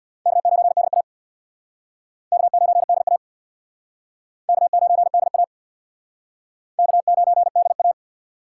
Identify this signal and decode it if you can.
Morse code, heard as a keyed tone, about 60 wpm, 700 Hz. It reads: K9DA K9DA K9DA K9DA
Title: Speed Racing - USA 1x2 Call Signs 50wpm